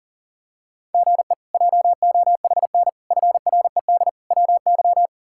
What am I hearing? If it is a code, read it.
Morse code, heard as a keyed tone, about 40 wpm, 700 Hz. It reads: GE JOHN FRED WY
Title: K1USN SST QSO Practice - 40wpm